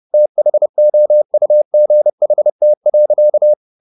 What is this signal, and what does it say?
Morse code, heard as a keyed tone, about 30 wpm, 600 Hz. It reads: THOUGHT.
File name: the old man and the sea / the old man and the sea 1 30wpm